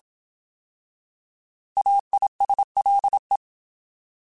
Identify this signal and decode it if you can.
Morse code, heard as a keyed tone, about 27 wpm, 800 Hz. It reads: AISLE
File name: Alphabetical / Alphabetical mp3s / As @25